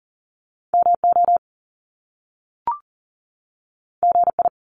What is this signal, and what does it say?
Morse code, heard as a keyed tone, about 39 wpm, 700 Hz. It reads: MO GI
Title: Top Two Letter Combinations in Words 40wpm